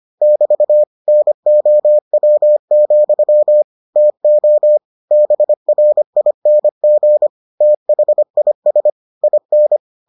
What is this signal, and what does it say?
Morse code, heard as a keyed tone, about 25 wpm, 600 Hz. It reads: <BT> NOW, TO BRING THIS IN